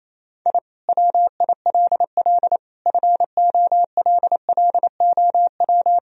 Morse code, heard as a keyed tone, about 28 wpm, 700 Hz. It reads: I WILL FOLLOW
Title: Sentences from Top 1000 Words - Review Part 1 - 28wpm